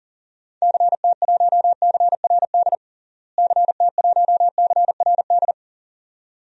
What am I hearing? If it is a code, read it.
Morse code, heard as a keyed tone, about 40 wpm, 700 Hz. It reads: CT1CRD CT1CRD
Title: DXCC-100 Call Signs - 6 Characters - Repeated 2x - 40wpm